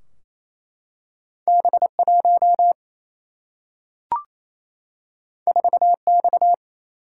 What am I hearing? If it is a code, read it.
Morse code, heard as a keyed tone, about 28 wpm, 700 Hz. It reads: B1 4X
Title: Single-Letter Single-Number 28wpm